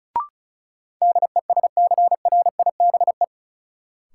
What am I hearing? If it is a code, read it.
Morse code, heard as a keyed tone, about 34 wpm, 700 Hz. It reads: DESCRIBE
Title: Sentences from Top 1000 Words - New Word Drill - 35wpm